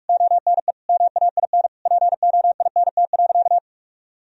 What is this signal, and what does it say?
Morse code, heard as a keyed tone, about 46 wpm, 700 Hz. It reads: ONE MAIN POINT.
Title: Sentences from Top 1000 Words - Review Part 2 - 45wpm